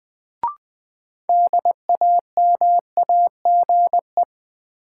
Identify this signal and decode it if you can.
Morse code, heard as a keyed tone, about 21 wpm, 700 Hz. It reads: DAMAGE
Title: Sentences from Top 1100 Words - New Word Drill - 20wpm